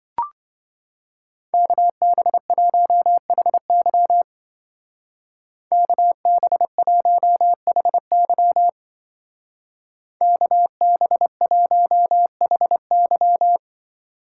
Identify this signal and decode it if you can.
Morse code, heard as a keyed tone, about 25 wpm, 700 Hz. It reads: KB1HY KB1HY KB1HY
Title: Speed Racing - USA 2x2 Call Signs 20wpm